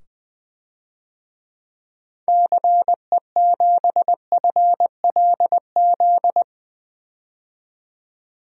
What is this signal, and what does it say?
Morse code, heard as a keyed tone, about 21 wpm, 700 Hz. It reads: CE7FLZ
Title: DXCC-100 Call Signs - 6 Characters - 20wpm